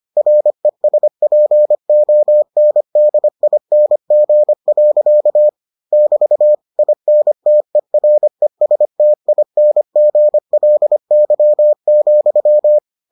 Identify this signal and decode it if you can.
Morse code, heard as a keyed tone, about 25 wpm, 600 Hz. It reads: RESPONDING. <BT> INTERESTINGLY,